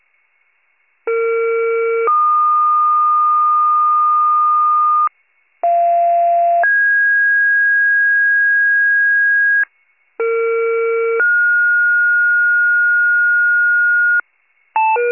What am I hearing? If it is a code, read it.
Two-tone page, heard as a one-second tone, then a three-second tone, about 460 then 1220 Hz, about 690 then 1650 Hz, about 460 then 1430 Hz, with noise. It is heard from 1.1 to 14.2 s.